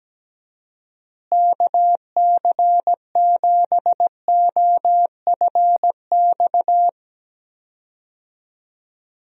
Morse code, heard as a keyed tone, about 17 wpm, 700 Hz. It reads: KC7OFX
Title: USA Call Signs - 6 Characters - Repeated 3x - 17wpm